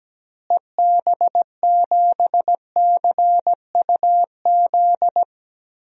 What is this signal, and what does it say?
Morse code, heard as a keyed tone, about 17 wpm, 700 Hz. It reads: EB7CUZ
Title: DXCC-100 Call Signs - 6 Characters - Repeated 2x - 17wpm